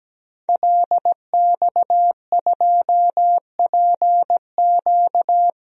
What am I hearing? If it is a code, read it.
Morse code, heard as a keyed tone, about 17 wpm, 700 Hz. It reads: LX2PQ
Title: DXCC-100 Call Signs - 5 Characters - 17wpm